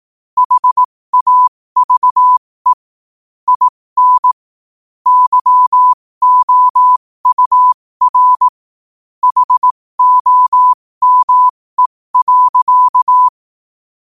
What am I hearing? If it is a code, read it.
Morse code, heard as a keyed tone, about 18 wpm, spaced as for 15 wpm, 1000 Hz. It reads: HAVE IN YOUR HOME.
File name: qotdfiles / 15wpm